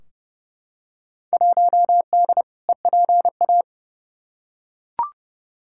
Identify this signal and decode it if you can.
Morse code, heard as a keyed tone, about 31 wpm, 700 Hz. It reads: 1D EPA
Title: Field Day Exchanges - 30wpm